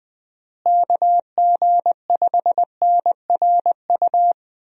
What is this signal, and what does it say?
Morse code, heard as a keyed tone, about 20 wpm, 700 Hz. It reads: KG5NRU
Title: USA Call Signs - 6 Characters - Repeated 3x - 20wpm